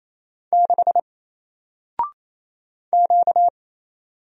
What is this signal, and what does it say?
Morse code, heard as a keyed tone, about 27 wpm, 700 Hz. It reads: 6 Q
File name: Single-Letter-Number-28wpm